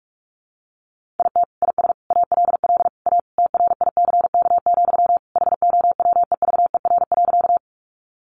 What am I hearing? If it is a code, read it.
Morse code, heard as a keyed tone, about 44 wpm, 700 Hz. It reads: IT IS ALL A TRICK, HOWEVER.
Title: Sentences from Top 1500 Words 45wpm